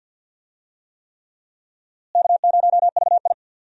Morse code, heard as a keyed tone, about 51 wpm, 700 Hz. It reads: K0VI
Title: USA Call Signs - 4 Characters - Repeated 3x - 50wpm